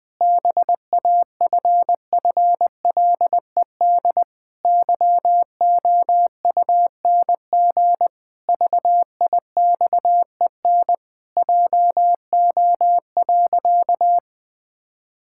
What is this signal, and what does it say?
Morse code, heard as a keyed tone, about 20 wpm, 700 Hz. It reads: BAFFLED YOUNG VIXEN JO.
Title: Pangram Challenge v2 - 20wpm